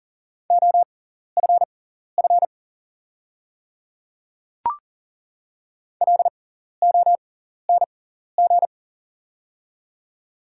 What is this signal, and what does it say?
Morse code, heard as a keyed tone, about 41 wpm, spaced as for 14 wpm, 690 Hz. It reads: OFF LONG
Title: 6x Character Spacing - Top 300 Words - 40wpm